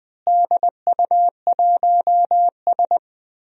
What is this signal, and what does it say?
Morse code, heard as a keyed tone, about 20 wpm, 700 Hz. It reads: DU1S
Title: DXCC-100 Call Signs - 4 Characters - Repeated 2x - 20wpm